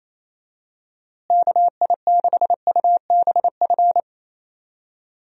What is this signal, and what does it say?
Morse code, heard as a keyed tone, about 28 wpm, 700 Hz. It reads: KI6UBF
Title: USA Call Signs - 6 Characters - Repeated 2x - 28wpm